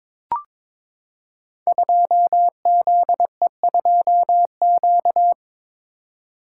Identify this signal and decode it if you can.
Morse code, heard as a keyed tone, about 22 wpm, 700 Hz. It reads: 2ZE2Q
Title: Code Groups - 5 Letter:Numbers - 22wpm